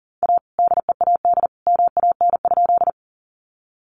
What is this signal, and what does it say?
Morse code, heard as a keyed tone, about 40 wpm, 700 Hz. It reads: A DEAD MAN?
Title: Sentences from Top 1500 Words 40wpm